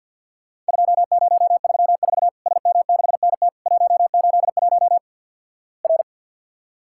Morse code, heard as a keyed tone, about 50 wpm, 700 Hz. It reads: 2034 SM6NT 181 R
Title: Worked All Europe DX Contest - QTC Exchange - 50wpm